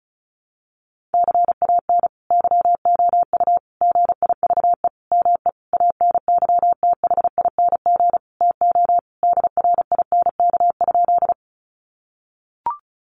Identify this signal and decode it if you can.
Morse code, heard as a keyed tone, about 34 wpm, 700 Hz. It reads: CAN YOU GIVE ME ANYTHING TO DRINK?
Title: Sentences from Top 1500 Words 35wpm